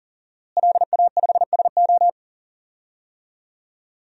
Morse code, heard as a keyed tone, about 40 wpm, 700 Hz. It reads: LA5SO
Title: DXCC-100 Call Signs - 5 Characters - Repeated 2x - 40wpm